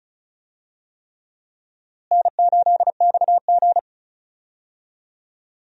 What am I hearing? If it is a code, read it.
Morse code, heard as a keyed tone, about 34 wpm, 700 Hz. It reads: N8XG
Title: USA Call Signs - 4 Characters - Repeated 3x - 35wpm